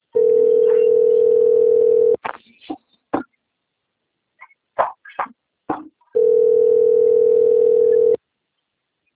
Ringback tone (North America), 440 + 480 Hz.